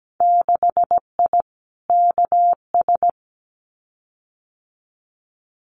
Morse code, heard as a keyed tone, about 17 wpm, 700 Hz. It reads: 6I KS
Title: Winter Field Day Exchanges 17wpm